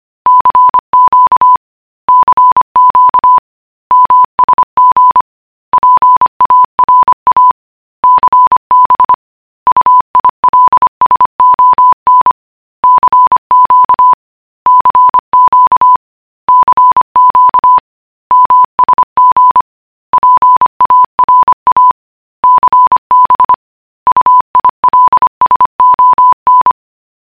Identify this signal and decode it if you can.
Morse code, heard as a keyed tone, about 25 wpm, 1000 Hz. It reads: CQ CQ MSG PARA CB UILSON CQ CQ CQ MSG PARA CB UILSON
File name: morse.cgi